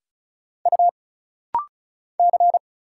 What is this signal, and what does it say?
Morse code, heard as a keyed tone, about 34 wpm, 700 Hz. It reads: U C
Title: Single Letter-Number Alt - Rapid-Fire 35wpm